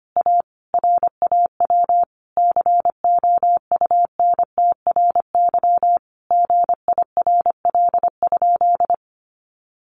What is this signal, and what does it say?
Morse code, heard as a keyed tone, about 25 wpm, 700 Hz. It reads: A RAW COUNTRY GIRL?
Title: Sentences from Top 1500 Words 25wpm